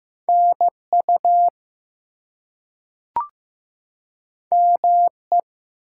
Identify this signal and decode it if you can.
Morse code, heard as a keyed tone, about 15 wpm, 700 Hz. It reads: NU ME
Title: US States and Canadian Provinces 15wpm